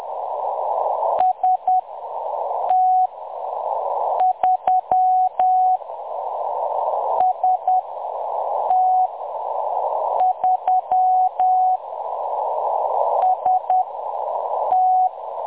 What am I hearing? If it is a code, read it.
Morse code, heard as a keyed tone, about 10 wpm, spaced as for 6 wpm, 740 Hz, with noise. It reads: ST3 ST3 ST
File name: M23-ST3- -0858utc-10184khz